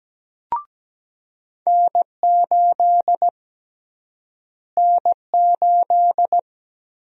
Morse code, heard as a keyed tone, about 17 wpm, 700 Hz. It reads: N8 N8
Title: USA Call Sign Prefixes - Repeated 2x - 17wpm